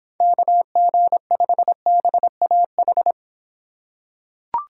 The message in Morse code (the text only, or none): KG5BAH